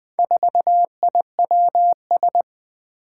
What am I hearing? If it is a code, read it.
Morse code, heard as a keyed tone, about 20 wpm, 700 Hz. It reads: 4IWS
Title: Four Challenging Letter-Number Combinations 20wpm